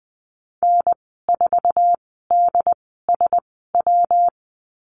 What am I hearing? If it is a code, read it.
Morse code, heard as a keyed tone, about 20 wpm, spaced as for 14 wpm, 700 Hz. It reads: N4DSW
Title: 231 Lesson 37g - Letter x - Call Signs 20wpm